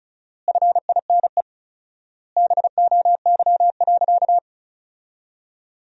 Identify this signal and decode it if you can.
Morse code, heard as a keyed tone, about 34 wpm, 700 Hz. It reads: FINE BOY.
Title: 4x Word-Spacing - Sentences from Top 500 Words - 35wpm